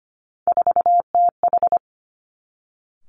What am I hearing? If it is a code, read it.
Morse code, heard as a keyed tone, about 25 wpm, 700 Hz. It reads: 4TH
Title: Three Challenging Letter-Number Combinations 25wpm